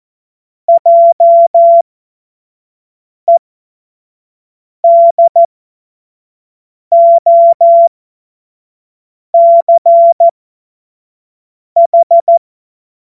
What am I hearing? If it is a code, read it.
Morse code, heard as a keyed tone, about 14 wpm, spaced as for 5 wpm, 680 Hz. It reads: JEDOCH
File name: Witz-23-September-2025-25-F70